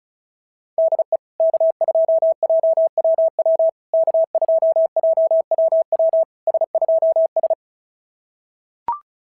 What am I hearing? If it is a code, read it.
Morse code, heard as a keyed tone, about 34 wpm, 650 Hz. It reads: DE K2JWW K2JWW S2S
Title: SOTA QSO Practice 35wpm